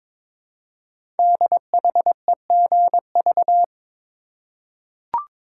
Morse code, heard as a keyed tone, about 23 wpm, 700 Hz. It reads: DHEGV
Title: Code Groups - 5 Letters - 22wpm